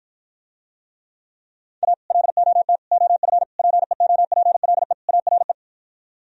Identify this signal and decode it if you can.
Morse code, heard as a keyed tone, about 52 wpm, 700 Hz. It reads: A LOT OF PEOPLE ARE